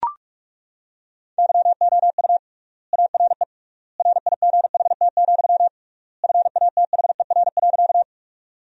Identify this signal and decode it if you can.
Morse code, heard as a keyed tone, about 42 wpm, 700 Hz. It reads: YOU ARE RIGHT, FATHER.
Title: 3x Word-Spacing - Sentences from Top 300 Words - 45wpm